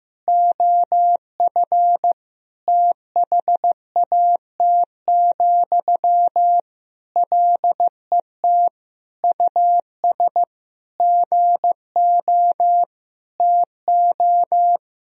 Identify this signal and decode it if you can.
Morse code, heard as a keyed tone, about 15 wpm, 700 Hz. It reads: OF THAT, LET US GO TO
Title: Sentences from Top 1000 Words - Review Part 4 - 15wpm